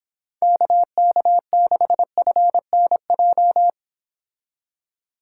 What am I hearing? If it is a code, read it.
Morse code, heard as a keyed tone, about 29 wpm, 700 Hz. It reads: KK6FNJ